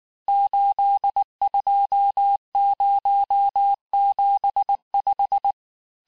Morse code, heard as a keyed tone, about 19 wpm, 800 Hz. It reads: 82075
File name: cours-du-lundi files / cours-1595-teste-CHIFFRE-vitesse18-MPM